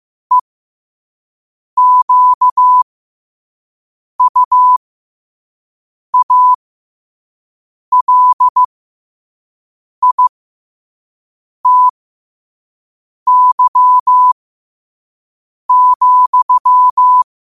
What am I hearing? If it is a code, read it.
Morse code, heard as a keyed tone, about 15 wpm, spaced as for 5 wpm, 1000 Hz. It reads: EQUALITY,
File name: qotdfiles / 5wpm